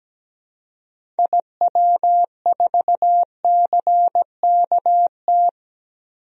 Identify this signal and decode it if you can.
Morse code, heard as a keyed tone, about 17 wpm, 700 Hz. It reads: IW4CKT
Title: DXCC-100 Call Signs - 6 Characters - 17wpm